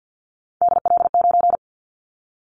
Morse code, heard as a keyed tone, about 50 wpm, 700 Hz. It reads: DL9
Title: Callsign Prefixes for Top 25 DXCC - 50wpm